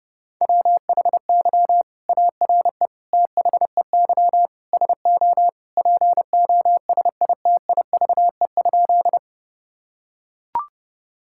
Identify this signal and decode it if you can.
Morse code, heard as a keyed tone, about 32 wpm, 700 Hz. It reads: WHY ARE THEY SO POSITIVE?
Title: Sentences from Top 1000 Words - Review Part 5 - 30wpm